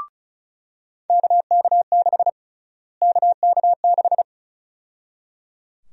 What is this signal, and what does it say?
Morse code, heard as a keyed tone, about 34 wpm, 700 Hz. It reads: KK6 KK6
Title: USA Call Sign Prefixes - Repeated 2x - 35wpm